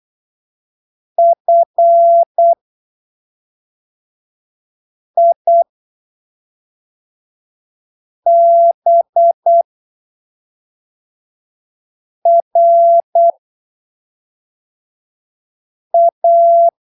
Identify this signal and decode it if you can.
Morse code, heard as a keyed tone, about 8 wpm, spaced as for 3 wpm, 680 Hz. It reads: FIBRA
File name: arquivos / texto / texto 11 08ppm